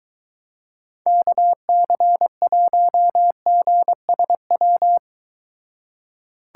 Morse code, heard as a keyed tone, about 23 wpm, 700 Hz. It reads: KC1GSW